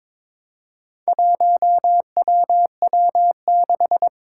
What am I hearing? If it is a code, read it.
Morse code, heard as a keyed tone, about 22 wpm, 700 Hz. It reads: 1WW6
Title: Four Challenging Letter-Number Combinations 22wpm